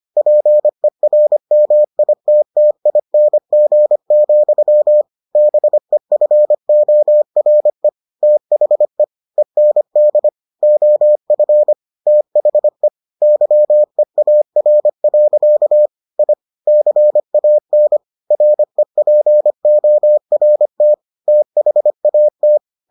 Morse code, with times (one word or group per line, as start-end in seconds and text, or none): 0.2-5.0 PERMITTING,
5.4-7.9 BEFORE
8.2-9.0 THE
9.4-10.3 END
10.6-11.7 OF
12.1-12.9 THE
13.2-15.9 YEAR.
16.2-16.3 I
16.7-18.0 CAN
18.3-20.9 REPORT
21.3-22.6 THAT